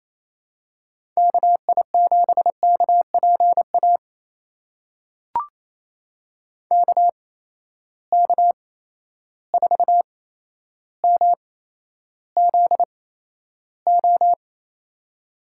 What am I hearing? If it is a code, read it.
Morse code, heard as a keyed tone, about 28 wpm, spaced as for 12 wpm, 700 Hz. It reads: KI7KPA KK4MZO
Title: USA Call Signs - 6 Characters - 8x Character Spacing - 28wpm